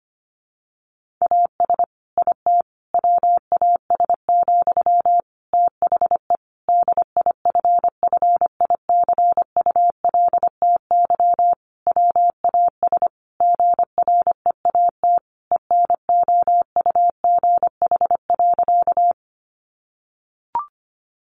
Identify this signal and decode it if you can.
Morse code, heard as a keyed tone, about 25 wpm, 700 Hz. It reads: AS IT WAS, THE DIFFICULTY WAS GREAT ENOUGH.
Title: Sentences from Top 1500 Words 25wpm